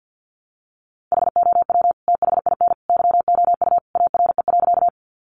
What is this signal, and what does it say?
Morse code, heard as a keyed tone, about 48 wpm, 700 Hz. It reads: HOW THIN YOU ARE.
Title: Sentences from Top 1200 Words 50wpm